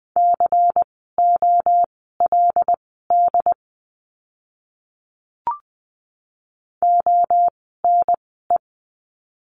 Morse code, heard as a keyed tone, about 20 wpm, spaced as for 15 wpm, 700 Hz. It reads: COLD ONE
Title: 094 Lesson 16f - Letter c - Words 20wpm